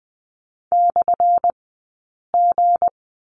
Morse code, heard as a keyed tone, about 20 wpm, spaced as for 8 wpm, 700 Hz. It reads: /G
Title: 183 Lesson 30d - Letter b - Two Characters 20wpm